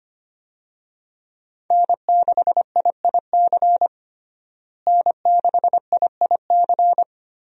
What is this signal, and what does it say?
Morse code, heard as a keyed tone, about 25 wpm, 700 Hz. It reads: N6IIC N6IIC
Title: USA Call Signs - 5 Characters - Repeated 3x - 25wpm